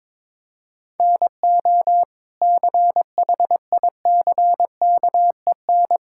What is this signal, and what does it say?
Morse code, heard as a keyed tone, about 21 wpm, 700 Hz. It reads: NO CHICKEN
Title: Sentences from Top 1000 Words - Review Part 5 - 22wpm